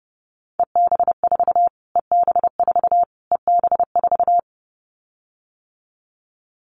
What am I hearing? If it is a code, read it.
Morse code, heard as a keyed tone, about 28 wpm, 700 Hz. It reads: EB4 EB4 EB4
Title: Callsign Prefixes for Top 25 DXCC - Repeated 3x - 30wpm